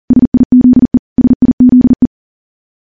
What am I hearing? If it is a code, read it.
Morse code, heard as a keyed tone, about 42 wpm, 250 Hz. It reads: SIZE SIZE
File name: morse / morse-training-40WPM